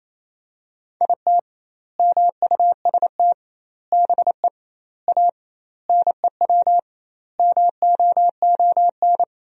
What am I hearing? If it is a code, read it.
Morse code, heard as a keyed tone, about 28 wpm, 700 Hz. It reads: IT MUST BE A NEW MOON
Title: 2x Word-Spacing - Sentences from Top 500 Words - 28wpm